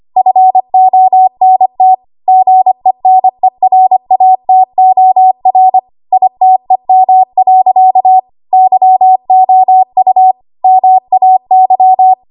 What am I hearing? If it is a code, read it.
Morse code, heard as a keyed tone, about 25 wpm, 750 Hz. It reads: FONT GENERATOR ITEM. YOU MAY
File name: Archive / 25 WPM / 25WPM